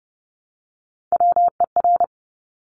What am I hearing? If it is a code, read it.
Morse code, heard as a keyed tone, about 30 wpm, 700 Hz. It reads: WER
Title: Top Three Letter Combinations in Words 30wpm